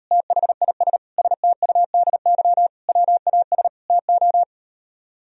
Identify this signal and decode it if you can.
Morse code, heard as a keyed tone, about 38 wpm, 700 Hz. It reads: THIS STUDY WAS TO